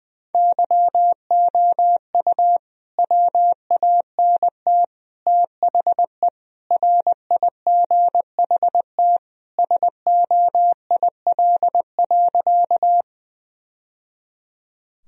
Morse code, heard as a keyed tone, about 21 wpm, 700 Hz. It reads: YOU WANT THE RIGHT SOIL.
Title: Sentences from Top 1000 Words - Review Part 4 - 20wpm